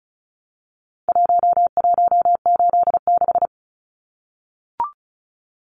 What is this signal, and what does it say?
Morse code, heard as a keyed tone, about 34 wpm, 700 Hz. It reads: 1186
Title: Numbers - Contest Style - 4-Digit 35wpm